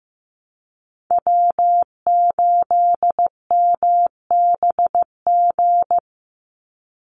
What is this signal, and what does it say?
Morse code, heard as a keyed tone, about 15 wpm, 700 Hz. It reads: W8MBG
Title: USA Call Signs 15wpm